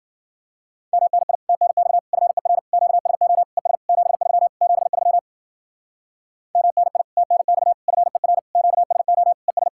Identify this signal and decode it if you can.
Morse code, heard as a keyed tone, about 48 wpm, 700 Hz. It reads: MNI TNX FER QSO ES 73 73 MNI TNX FER QSO ES
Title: Speed-Racing Common QSO Phrases - 40wpm